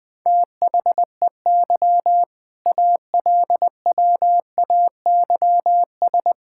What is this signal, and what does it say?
Morse code, heard as a keyed tone, about 20 wpm, 700 Hz. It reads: THEY ALWAYS
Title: Sentences with 6-Word Phrases - 20wpm